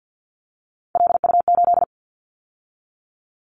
Morse code, heard as a keyed tone, about 50 wpm, 700 Hz. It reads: LU8
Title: Callsign Prefixes for Top 50 DXCC - 50wpm